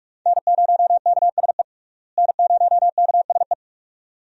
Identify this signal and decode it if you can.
Morse code, heard as a keyed tone, about 46 wpm, 700 Hz. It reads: N0KSE N0KSE
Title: USA Call Signs - 5 Characters - Repeated 2x - 45wpm